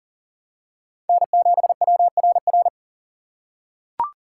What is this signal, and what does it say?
Morse code, heard as a keyed tone, about 40 wpm, 700 Hz. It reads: N7WRR
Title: USA Call Signs - 4x Character Spacing - 40wpm